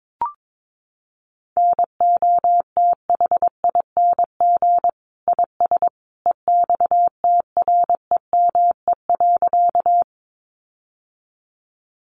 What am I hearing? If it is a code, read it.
Morse code, heard as a keyed tone, about 22 wpm, 700 Hz. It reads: NOTHING IS EXTREME.
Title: Sentences from Top 1300 Words 22wpm